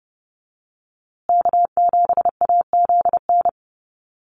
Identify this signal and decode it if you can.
Morse code, heard as a keyed tone, about 30 wpm, 700 Hz. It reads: K7AZN
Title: USA Call Signs Repeated 2x 30wpm